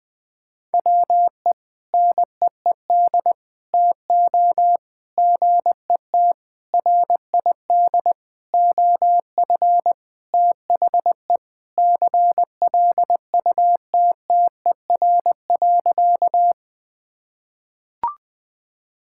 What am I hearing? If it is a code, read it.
Morse code, heard as a keyed tone, about 20 wpm, 700 Hz. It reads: WE NEED TO GET RID OF THE CLUTTER.
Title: Sentences with 5-Word Phrases - 20wpm